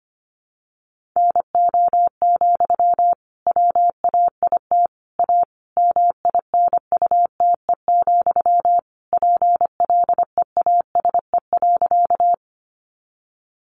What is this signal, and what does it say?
Morse code, heard as a keyed tone, about 25 wpm, 700 Hz. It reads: NO, WAIT A MINUTE, PLEASE.
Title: Sentences from Top 1500 Words 25wpm